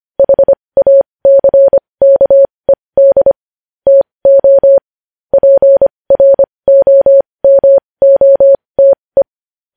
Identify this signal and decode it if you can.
Morse code, heard as a keyed tone, about 25 wpm, spaced as for 20 wpm, 550 Hz. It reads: HACKED TO PROMOTE
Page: news-headlines-in-morse-code-at-20-wpm